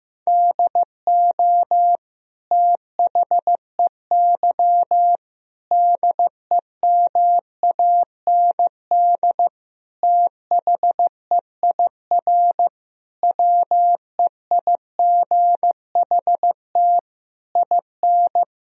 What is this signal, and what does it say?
Morse code, heard as a keyed tone, about 15 wpm, 700 Hz. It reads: DO THEY DEMAND THEIR WEIGHT IN
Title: Sentences from Top 1000 Words - Review Part 3 - 15wpm